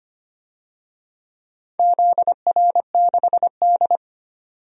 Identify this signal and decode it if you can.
Morse code, heard as a keyed tone, about 25 wpm, 700 Hz. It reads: ZR6D
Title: DXCC-100 Call Signs - 4 Characters - 25wpm